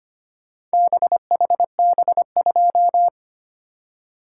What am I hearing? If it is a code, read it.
Morse code, heard as a keyed tone, about 25 wpm, 700 Hz. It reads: BHB2
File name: practice / 25wpm / Four Letter-Number Combinations - 25wpm